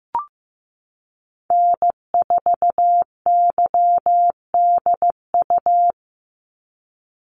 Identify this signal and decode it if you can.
Morse code, heard as a keyed tone, about 15 wpm, 700 Hz. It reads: N4YDU
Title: Call Signs Repeated 3x - 15wpm